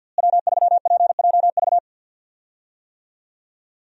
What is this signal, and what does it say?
Morse code, heard as a keyed tone, about 48 wpm, 700 Hz. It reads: W3PJV
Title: USA Call Signs - 5 Characters - Repeated 3x - 50wpm